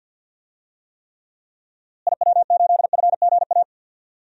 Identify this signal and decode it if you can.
Morse code, heard as a keyed tone, about 46 wpm, 700 Hz. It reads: IW8FGA
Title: DXCC-100 Call Signs - 6 Characters - Repeated 3x - 50wpm